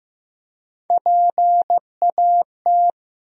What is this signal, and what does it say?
Morse code, heard as a keyed tone, about 15 wpm, 700 Hz. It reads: PAT
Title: USA Common Names - Max 3 Letters - 15wpm